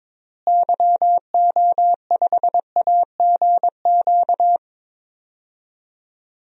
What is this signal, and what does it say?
Morse code, heard as a keyed tone, about 22 wpm, 700 Hz. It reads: YO5AGQ
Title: DXCC-100 Call Signs - 6 Characters - Repeated 2x - 22wpm